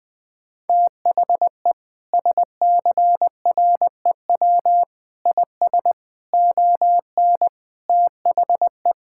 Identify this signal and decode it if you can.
Morse code, heard as a keyed tone, about 21 wpm, 700 Hz. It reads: THE SCREW IS ON THE
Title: Sentences from Top 1600 Words 20wpm